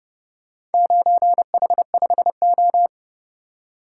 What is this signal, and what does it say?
Morse code, heard as a keyed tone, about 29 wpm, 700 Hz. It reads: 9H5O